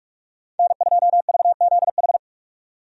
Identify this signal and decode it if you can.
Morse code, heard as a keyed tone, about 41 wpm, 700 Hz. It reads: N2VZH
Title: USA Call Signs - 5 Characters - 45wpm